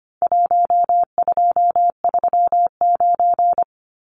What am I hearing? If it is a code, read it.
Morse code, heard as a keyed tone, about 25 wpm, 700 Hz. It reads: 1239
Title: Numbers - 4-Digit 25wpm